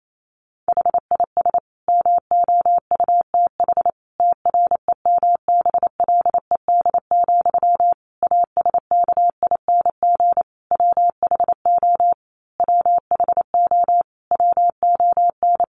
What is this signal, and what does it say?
Morse code, heard as a keyed tone, about 28 wpm, 700 Hz. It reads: HIS MOUTH TREMBLED, ASKING WHO WHO WON
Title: A Sound Of Thunder 28wpm